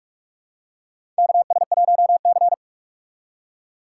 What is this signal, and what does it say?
Morse code, heard as a keyed tone, about 46 wpm, 700 Hz. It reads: KS1C
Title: USA Call Signs - 6x Character Spacing - 45wpm